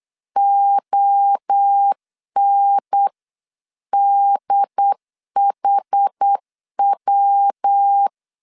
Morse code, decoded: ONDHW

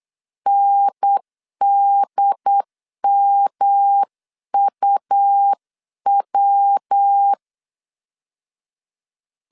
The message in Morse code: NDMUW

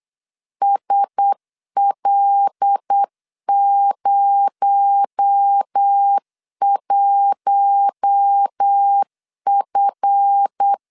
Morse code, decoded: SL01F

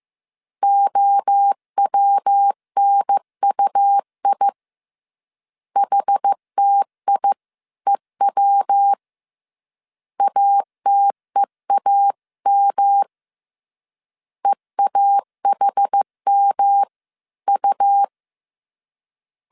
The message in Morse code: OWNUI HTIEW ATEAM EAHMU

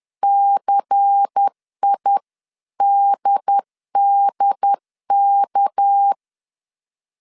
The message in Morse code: CIDDK

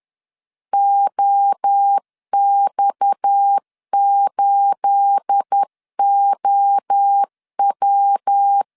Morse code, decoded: OX8OW

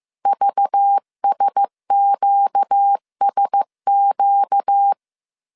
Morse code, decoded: VSQSQ

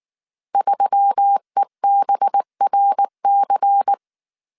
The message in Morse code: 3EBRC

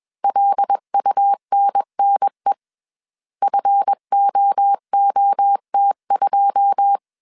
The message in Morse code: LUNNE FOOT2

